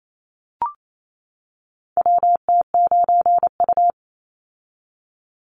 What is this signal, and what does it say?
Morse code, heard as a keyed tone, about 28 wpm, 700 Hz. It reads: WT9U
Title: Call Signs - 28wpm